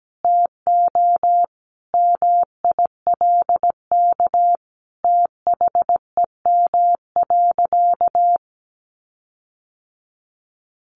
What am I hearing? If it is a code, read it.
Morse code, heard as a keyed tone, about 17 wpm, 700 Hz. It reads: TO MILK THEM.
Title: Sentences from Top 1500 Words 17wpm